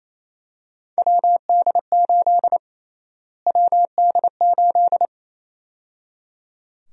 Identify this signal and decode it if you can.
Morse code, heard as a keyed tone, about 28 wpm, 700 Hz. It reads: WD8 WD8
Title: USA Call Sign Prefixes - Repeated 2x - 28wpm